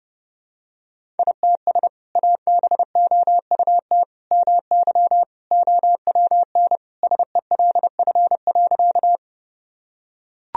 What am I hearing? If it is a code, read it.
Morse code, heard as a keyed tone, about 32 wpm, 700 Hz. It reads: ITS ABOUT MY OWN SELF.
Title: Sentences from Top 1000 Words - Review Part 1 - 30wpm